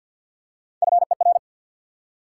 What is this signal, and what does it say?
Morse code, heard as a keyed tone, about 53 wpm, 700 Hz. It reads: FER
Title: Top 90 QSO Elements 50wpm